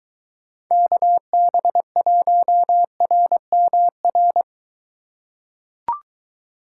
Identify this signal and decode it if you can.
Morse code, heard as a keyed tone, about 23 wpm, 700 Hz. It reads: KB1RMR